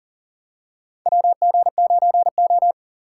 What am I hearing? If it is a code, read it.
Morse code, heard as a keyed tone, about 37 wpm, 700 Hz. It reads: WG9O
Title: USA Call Signs - 4 Characters - 4x Character Spacing - 40wpm